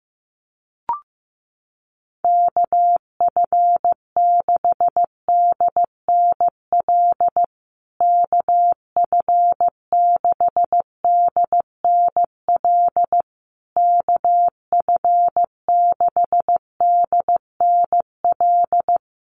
Morse code, heard as a keyed tone, about 15 wpm, 700 Hz. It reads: KF6DNL KF6DNL KF6DNL
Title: USA Call Signs Repeated 3x 15wpm